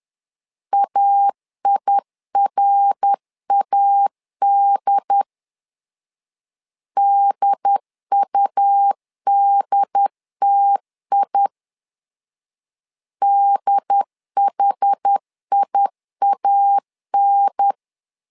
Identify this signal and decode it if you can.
Morse code, heard as a keyed tone, about 10 wpm, 800 Hz. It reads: AIRAD DUDTI DHIAN
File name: LEZIONI / 4 Q 10wpm 2210